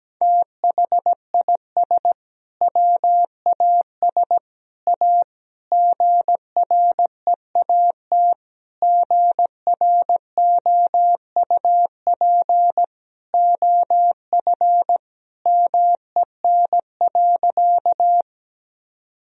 Morse code, heard as a keyed tone, about 17 wpm, 700 Hz. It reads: THIS WAS A GREAT GROUP OF MEN.
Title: Sentences from Top 1000 Words - Review Part 1 - 17wpm